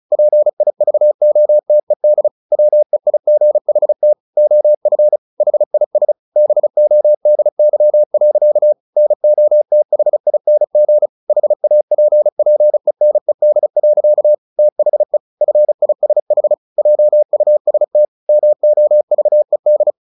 Morse code, heard as a keyed tone, about 34 wpm, 600 Hz. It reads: PIVOTED WEIGHT OF HIS BODY. NOTHING HAPPENED. THE FISH JUST MOVED
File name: the old man and the sea / the old man and the sea 1 35wpm